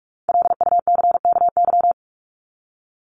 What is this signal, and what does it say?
Morse code, heard as a keyed tone, about 44 wpm, 700 Hz. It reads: LUCKY
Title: Top 1751-2000 Words 45wpm